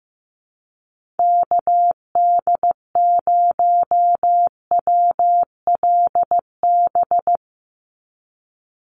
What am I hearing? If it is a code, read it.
Morse code, heard as a keyed tone, about 15 wpm, 700 Hz. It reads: KD0WLB
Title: USA Call Signs 15wpm